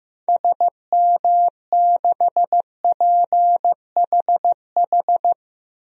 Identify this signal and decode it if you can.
Morse code, heard as a keyed tone, about 16 wpm, 700 Hz. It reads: SM6PHH
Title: DXCC-100 Call Signs - 6 Characters - Repeated 2x - 15wpm